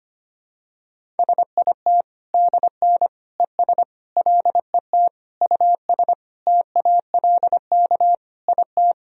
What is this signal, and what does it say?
Morse code, heard as a keyed tone, about 25 wpm, 700 Hz. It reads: SIT DN ES LET US TALK IT
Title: Sentences from Top 500 Words and Abbreviations 25wpm